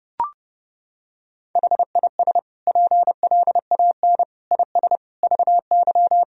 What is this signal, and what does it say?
Morse code, heard as a keyed tone, about 30 wpm, 700 Hz. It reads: HIS PLAN IS VY
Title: Sentences from Top 500 Words and Abbreviations 30wpm